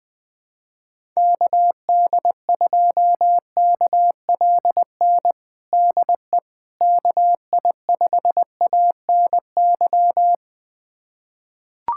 Morse code, heard as a keyed tone, about 20 wpm, 700 Hz. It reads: KD2KLN DE KI5ANY
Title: USA Callsign DE Callsign - 6 Characters - 20wpm